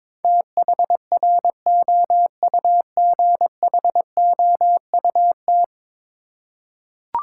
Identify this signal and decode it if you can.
Morse code, heard as a keyed tone, about 21 wpm, 700 Hz. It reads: THROUGHOUT
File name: practice / 22wpm / 10 Letter Words - First 3 Letters Given - 22wpm